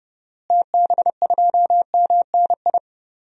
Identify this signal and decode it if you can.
Morse code, heard as a keyed tone, about 30 wpm, 700 Hz. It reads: TB2MNI
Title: DXCC-100 Call Signs - 6 Characters - Repeated 2x - 30wpm